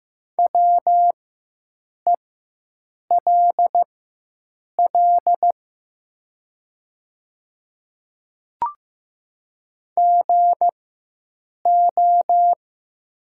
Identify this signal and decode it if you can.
Morse code, heard as a keyed tone, about 15 wpm, spaced as for 7 wpm, 700 Hz. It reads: WELL GO
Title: 4x Character Spacing - Top 300 Words - 15wpm